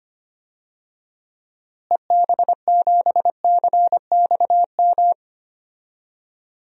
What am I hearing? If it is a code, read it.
Morse code, heard as a keyed tone, about 25 wpm, 700 Hz. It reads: EB7CXM
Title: DXCC-100 Call Signs - 6 Characters - Repeated 2x - 25wpm